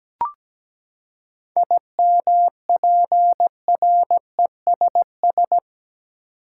Morse code, heard as a keyed tone, about 17 wpm, 700 Hz. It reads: IMPRESS
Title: Seven Letter Words - 17wpm